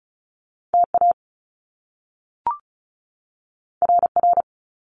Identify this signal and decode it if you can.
Morse code, heard as a keyed tone, about 36 wpm, 700 Hz. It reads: TA RR
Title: Top Two Letter Combinations in Words 35wpm